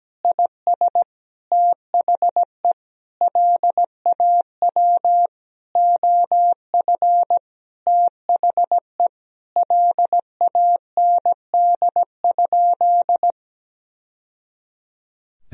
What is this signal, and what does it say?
Morse code, heard as a keyed tone, about 17 wpm, 700 Hz. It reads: IS THE LAW OF THE LAND?
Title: Sentences from Top 1000 Words - Review Part 3 - 17wpm